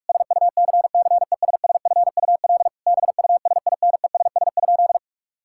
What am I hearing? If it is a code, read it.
Morse code, heard as a keyed tone, about 41 wpm, 700 Hz. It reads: SUCCESSFUL BUSINESS?
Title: Sentences from Top 1000 Words - Review Part 6 - 45wpm